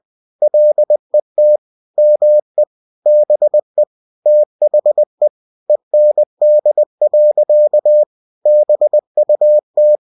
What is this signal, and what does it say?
Morse code, heard as a keyed tone, about 20 wpm, 600 Hz. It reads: LET ME BE THE END. BUT
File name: cw-books / winnie-the-pooh / winnie-20wpm-page-0079